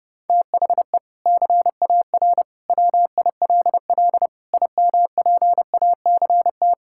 Morse code, heard as a keyed tone, about 32 wpm, 700 Hz. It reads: THE CAR WILL IMPACT
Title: Sentences from Top 1000 Words - Review Part 6 - 30wpm